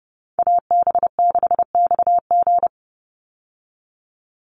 Morse code, heard as a keyed tone, about 30 wpm, 700 Hz. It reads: AB6XG
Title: Call Signs Repeated 2x - 30wpm